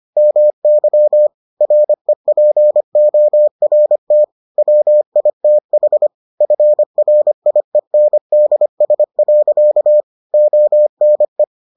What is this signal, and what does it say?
Morse code, heard as a keyed tone, about 25 wpm, 600 Hz. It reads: MY REPORT WITH FRIENDS. ONE